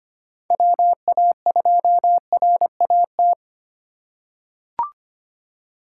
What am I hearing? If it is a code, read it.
Morse code, heard as a keyed tone, about 25 wpm, 700 Hz. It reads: WA2RAT
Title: USA Call Signs - 6 Characters - 4x Character Spacing - 25wpm